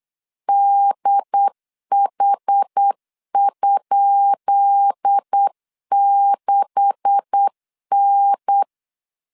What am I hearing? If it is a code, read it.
Morse code, heard as a keyed tone, about 8 wpm, 800 Hz. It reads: DH?6N
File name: LEZIONI / 10 SIGNS / 10B INT 8wpm 3534